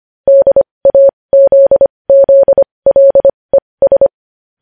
Morse code, heard as a keyed tone, about 25 wpm, spaced as for 20 wpm, 550 Hz. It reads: DAZZLES
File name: podcast / News headlines-1x-25x20